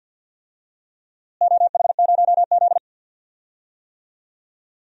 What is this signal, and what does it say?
Morse code, heard as a keyed tone, about 52 wpm, 700 Hz. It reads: OH0Z
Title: DXCC-100 Call Signs - 4 Characters - Repeated 2x - 50wpm